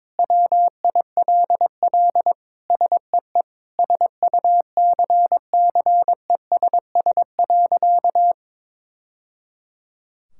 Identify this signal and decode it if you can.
Morse code, heard as a keyed tone, about 22 wpm, 700 Hz. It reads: WILL SEE SUCCESS.
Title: Sentences from Top 1000 Words - Review Part 5 - 22wpm